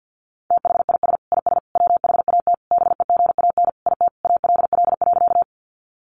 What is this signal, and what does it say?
Morse code, heard as a keyed tone, about 49 wpm, 700 Hz. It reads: THIS IS WHAT BEGAN IT ALL.
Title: Sentences from Top 300 Words 50wpm